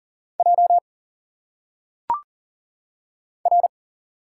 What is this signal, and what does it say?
Morse code, heard as a keyed tone, about 40 wpm, 700 Hz. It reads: J R
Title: Single Letters - Phonetic Pronunciation 40wpm